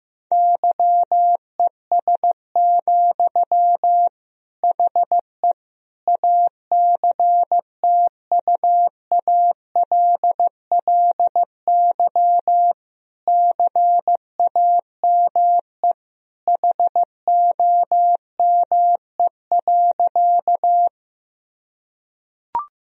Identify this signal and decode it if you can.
Morse code, heard as a keyed tone, about 15 wpm, 700 Hz. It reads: YES, HE ACTUALLY CAME HOME.
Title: Sentences from Top 1000 Words - Review Part 4 - 15wpm